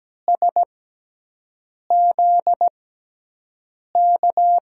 Morse code, heard as a keyed tone, about 17 wpm, spaced as for 6 wpm, 700 Hz. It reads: SZK
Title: USA Call Signs - 6 Characters - 6x Character Spacing - 17wpm